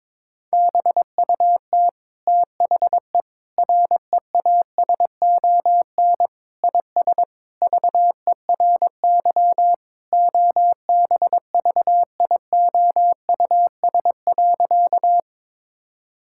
Morse code, heard as a keyed tone, about 22 wpm, 700 Hz. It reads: BUT THE REASON IS VERY OBVIOUS.
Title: Sentences from Top 1600 Words 22wpm